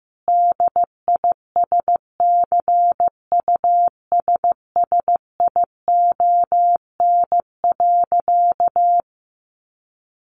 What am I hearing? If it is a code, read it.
Morse code, heard as a keyed tone, about 15 wpm, 700 Hz. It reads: DISCUSSION.
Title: Sentences from Top 1500 Words 15wpm